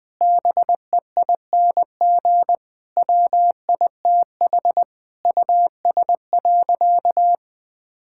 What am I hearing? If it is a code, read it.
Morse code, heard as a keyed tone, about 20 wpm, 700 Hz. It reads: BEING WITH US.
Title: Sentences with 5-Word Phrases - 20wpm